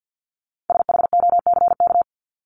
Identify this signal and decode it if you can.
Morse code, heard as a keyed tone, about 50 wpm, 700 Hz. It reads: SHOCK
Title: Top 1751-2000 Words 50wpm